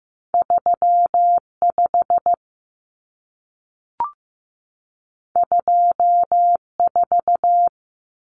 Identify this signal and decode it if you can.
Morse code, heard as a keyed tone, about 15 wpm, 700 Hz. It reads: 35 24
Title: Common Cut Numbers - 2-Digit 15wpm